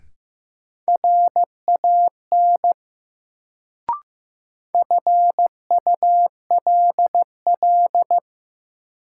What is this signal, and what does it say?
Morse code, heard as a keyed tone, about 15 wpm, 700 Hz. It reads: RAN FULL
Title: Sentences from Top 500 Words - New Word Drill - 15wpm